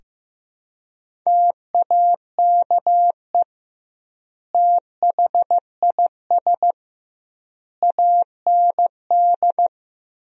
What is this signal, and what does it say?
Morse code, heard as a keyed tone, about 15 wpm, 700 Hz. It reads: TAKE THIS AND
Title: 2x Word-Spacing - Sentences from Top 200 Words - 15wpm